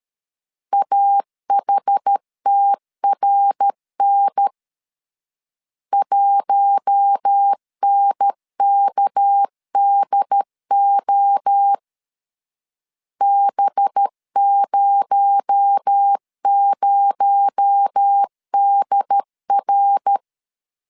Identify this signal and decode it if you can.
Morse code, heard as a keyed tone, about 13 wpm, 800 Hz. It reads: AHTRN 1NKDO B00DR
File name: CW / DET / 1 6 1500C S B 12wpm 3431